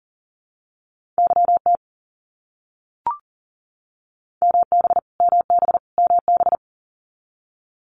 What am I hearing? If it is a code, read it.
Morse code, heard as a keyed tone, about 38 wpm, 700 Hz. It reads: YT MB MB MB
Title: Canadian Provinces and Territories - Repeated 3x - 40wpm